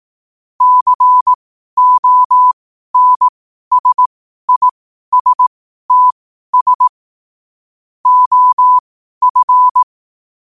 Morse code, heard as a keyed tone, about 18 wpm, spaced as for 12 wpm, 1000 Hz. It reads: CONSISTS OF
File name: qotdfiles / 12wpm